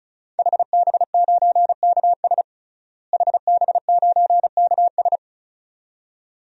Morse code, heard as a keyed tone, about 36 wpm, 700 Hz. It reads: HB9KS HB9KS
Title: DXCC-100 Call Signs - 5 Characters - Repeated 2x - 35wpm